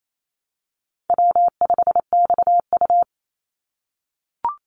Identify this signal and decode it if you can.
Morse code, heard as a keyed tone, about 28 wpm, 700 Hz. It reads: W5XU
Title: Call Signs - 28wpm